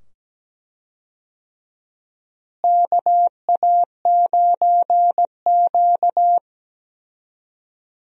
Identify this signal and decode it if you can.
Morse code, heard as a keyed tone, about 17 wpm, 700 Hz. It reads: KA9Q
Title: USA Call Signs - 4 Characters - Repeated 2x - 17wpm